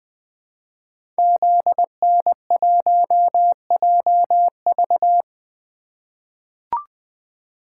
Morse code, heard as a keyed tone, about 20 wpm, 700 Hz. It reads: ZN1JV
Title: Code Groups - 5 Letter:Numbers - 20wpm